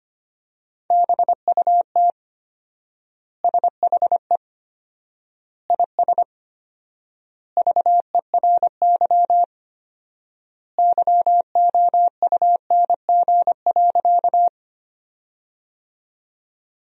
Morse code, heard as a keyed tone, about 25 wpm, 700 Hz. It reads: BUT SHE IS VERY YOUNG.
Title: 4x Word-Spacing - Sentences from Top 400 Words - 25wpm